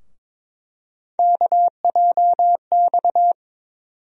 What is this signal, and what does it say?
Morse code, heard as a keyed tone, about 21 wpm, 700 Hz. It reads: KJX
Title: USA Call Sign Postfixes - 3 Letters 22wpm